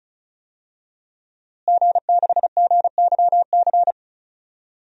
Morse code, heard as a keyed tone, about 34 wpm, 700 Hz. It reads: G6GYC
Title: DXCC-100 Call Signs - 5 Characters - Repeated 2x - 35wpm